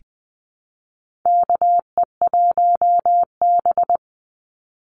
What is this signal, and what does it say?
Morse code, heard as a keyed tone, about 20 wpm, 700 Hz. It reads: KE1B
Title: Call Signs Repeated 2x - 20wpm